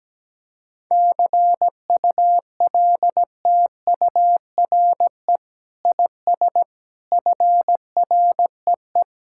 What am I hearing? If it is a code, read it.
Morse code, heard as a keyed tone, about 17 wpm, 700 Hz. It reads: CULTURE IS FREE
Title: Sentences from Top 1000 Words - Review Part 6 - 17wpm